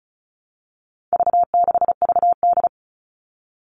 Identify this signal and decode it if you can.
Morse code, heard as a keyed tone, about 34 wpm, 700 Hz. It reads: V6VD
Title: All Cut Numbers - 4-Digit 35wpm